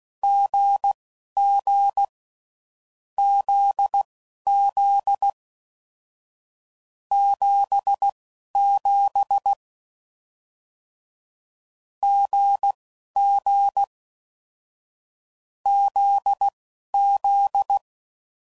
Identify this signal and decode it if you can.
Morse code, heard as a keyed tone, about 16 wpm, spaced as for 11 wpm, 800 Hz. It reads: GG ZZ 77 GG ZZ